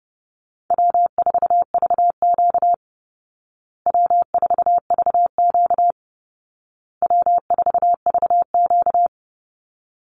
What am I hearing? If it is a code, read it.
Morse code, heard as a keyed tone, about 30 wpm, 700 Hz. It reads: W4VQ W4VQ W4VQ
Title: Easy Call Signs Repeated 3x - 30wpm